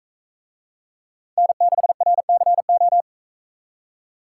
Morse code, heard as a keyed tone, about 41 wpm, 700 Hz. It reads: N6RCO